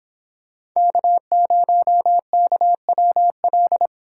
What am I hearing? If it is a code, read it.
Morse code, heard as a keyed tone, about 24 wpm, 700 Hz. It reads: K0KWL